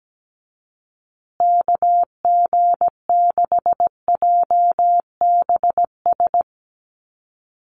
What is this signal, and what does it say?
Morse code, heard as a keyed tone, about 17 wpm, 700 Hz. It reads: KG6JBS
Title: USA Call Signs Repeated 3x 17wpm